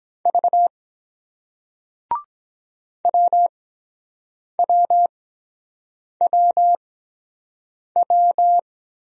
Morse code, heard as a keyed tone, about 23 wpm, spaced as for 7 wpm, 700 Hz. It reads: V WWWW